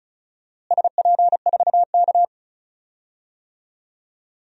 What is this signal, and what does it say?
Morse code, heard as a keyed tone, about 34 wpm, 700 Hz. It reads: SP4K
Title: DXCC-100 Call Signs - 4 Characters - Repeated 3x - 35wpm